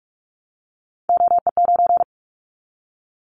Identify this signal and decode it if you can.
Morse code, heard as a keyed tone, about 44 wpm, 700 Hz. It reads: OE9
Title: Callsign Prefixes for Top 25 DXCC - 45wpm